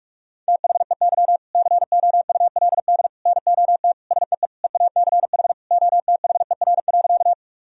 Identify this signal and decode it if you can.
Morse code, heard as a keyed tone, about 46 wpm, 700 Hz. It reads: THEY COULD NOT SEE EACH OTHER.
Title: Sentences from Top 1000 Words - Review Part 1 - 45wpm